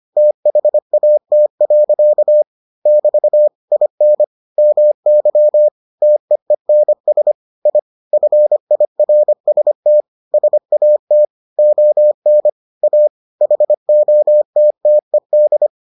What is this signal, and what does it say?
Morse code, heard as a keyed tone, about 25 wpm, 600 Hz. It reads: THAT. <BT> IN MY TEENS I FIRST SAT ON A HOTTED